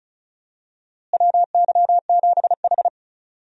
Morse code, heard as a keyed tone, about 36 wpm, 700 Hz. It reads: WY7H
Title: USA Call Signs - 4 Characters - Repeated 3x - 35wpm